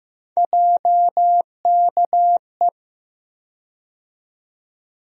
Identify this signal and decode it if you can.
Morse code, heard as a keyed tone, about 15 wpm, 700 Hz. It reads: JKE
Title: USA Call Sign Postfixes - 3 Letters 15wpm